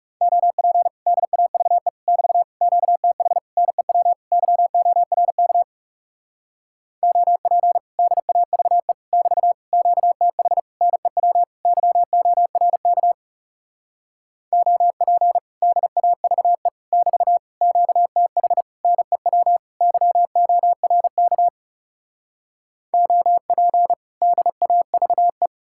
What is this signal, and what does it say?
Morse code, heard as a keyed tone, about 40 wpm, 700 Hz. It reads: OP DAVE <BT> QTH NEW YORK OP DAVE <BT> QTH NEW YORK OP DAVE <BT> QTH NEW YORK OP DAVE